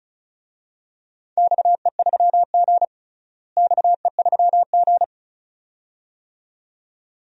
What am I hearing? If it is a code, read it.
Morse code, heard as a keyed tone, about 34 wpm, 700 Hz. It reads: XE3G XE3G
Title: DXCC-100 Call Signs - 4 Characters - Repeated 2x - 35wpm